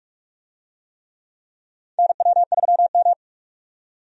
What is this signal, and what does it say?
Morse code, heard as a keyed tone, about 45 wpm, 700 Hz. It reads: NW3M
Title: USA Call Signs - 4 Characters - 45wpm